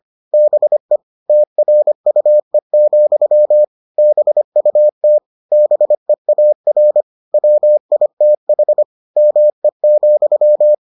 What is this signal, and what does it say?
Morse code, heard as a keyed tone, about 25 wpm, 600 Hz. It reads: BE TRUE, BUT BEAR WITH ME,